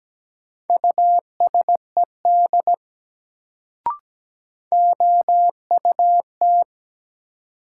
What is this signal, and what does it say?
Morse code, heard as a keyed tone, about 17 wpm, 700 Hz. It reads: USED OUT
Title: Sets of 2 Words - New Word Drill - 17wpm